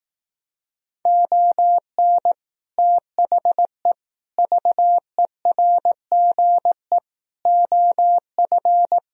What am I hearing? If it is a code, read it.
Morse code, heard as a keyed tone, about 18 wpm, 700 Hz. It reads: ON THE VERGE OF
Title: Speed-Racing - Sets of 4 Words 15wpm